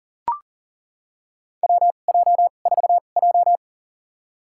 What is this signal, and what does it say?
Morse code, heard as a keyed tone, about 40 wpm, spaced as for 29 wpm, 700 Hz. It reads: WJ4J
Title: USA Call Signs - 2x Character Spacing - 40wpm